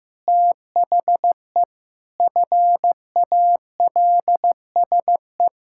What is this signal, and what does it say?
Morse code, heard as a keyed tone, about 14 wpm, 700 Hz. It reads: THE FALSE
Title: Sentences from Top 1600 Words 15wpm